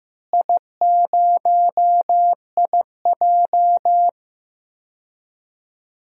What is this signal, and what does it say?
Morse code, heard as a keyed tone, about 15 wpm, 700 Hz. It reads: I0IJ
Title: DXCC-100 Call Signs - 4 Characters - Repeated 2x - 15wpm